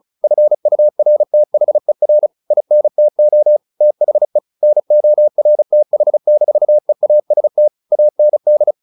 Morse code, heard as a keyed tone, about 34 wpm, 600 Hz. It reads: FURTHER INTO THE NORTH-EAST AND
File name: the old man and the sea 2 35wpm